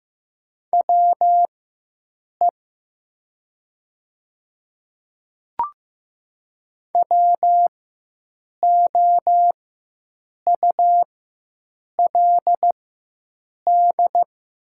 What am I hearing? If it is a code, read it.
Morse code, heard as a keyed tone, about 15 wpm, spaced as for 7 wpm, 700 Hz. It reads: WE WOULD
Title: 4x Character Spacing - Top 100 Words - 15wpm